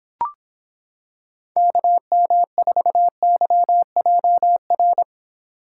Morse code, heard as a keyed tone, about 26 wpm, 700 Hz. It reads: KM4YJR